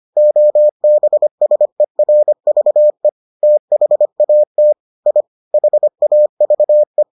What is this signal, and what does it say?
Morse code, heard as a keyed tone, about 25 wpm, 600 Hz. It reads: OBSERVE THAT I HAVE